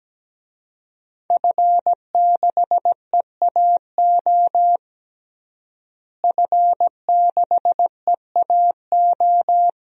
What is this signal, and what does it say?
Morse code, heard as a keyed tone, about 17 wpm, 700 Hz. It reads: F6EAO F6EAO
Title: DXCC-100 Call Signs - 5 Characters - Repeated 2x - 17wpm